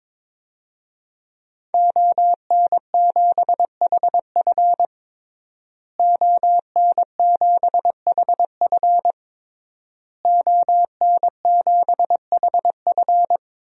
Morse code, heard as a keyed tone, about 22 wpm, 700 Hz. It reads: ON7HF ON7HF ON7HF
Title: DXCC-100 Call Signs - 5 Characters - Repeated 3x - 22wpm